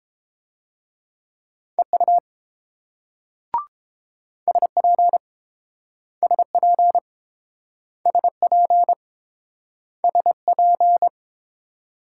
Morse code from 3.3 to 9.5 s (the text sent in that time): SP SP SP